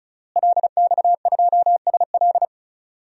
Morse code, heard as a keyed tone, about 34 wpm, 700 Hz. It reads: LX2SL
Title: DXCC-100 Call Signs - 5 Characters - Repeated 3x - 35wpm